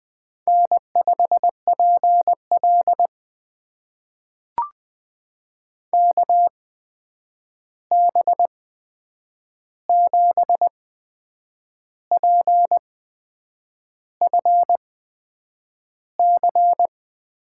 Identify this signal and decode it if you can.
Morse code, heard as a keyed tone, about 20 wpm, spaced as for 8 wpm, 700 Hz. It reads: N5PL KB7PFC
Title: USA Call Signs - 8x Character Spacing - 20wpm